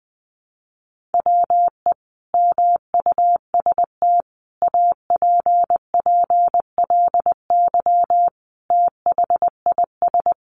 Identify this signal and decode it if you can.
Morse code, heard as a keyed tone, about 20 wpm, 700 Hz. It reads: WE MUST APPLY THIS
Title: Sentences from Top 1400 Words 20wpm